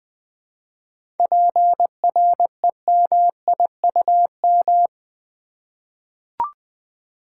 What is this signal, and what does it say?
Morse code, heard as a keyed tone, about 20 wpm, 700 Hz. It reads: PREMIUM
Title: 7 Letter Words - First 3 Letters Given - 20wpm